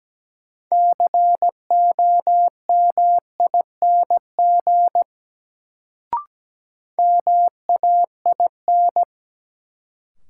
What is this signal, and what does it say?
Morse code, heard as a keyed tone, about 17 wpm, 700 Hz. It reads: COMING MAIN
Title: Sentences from Top 1300 Words - New Word Drill Alt - 17wpm